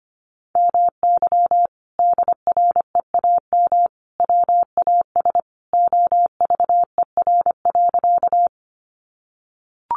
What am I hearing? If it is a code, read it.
Morse code, heard as a keyed tone, about 25 wpm, 700 Hz. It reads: MY DREAM WAS OVER.
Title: Sentences from Top 1500 Words 25wpm